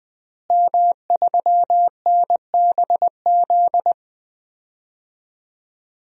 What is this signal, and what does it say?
Morse code, heard as a keyed tone, about 20 wpm, 700 Hz. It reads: M3NBZ
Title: DXCC-100 Call Signs - 5 Characters - Repeated 2x - 20wpm